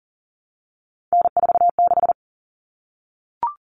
Morse code, heard as a keyed tone, about 40 wpm, 700 Hz. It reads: N46
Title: Common Cut Numbers - 3-Digit 40wpm